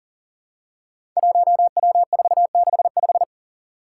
Morse code, heard as a keyed tone, about 41 wpm, 700 Hz. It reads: 1W465
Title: Five Challenging Letter-Number Combinations 40wpm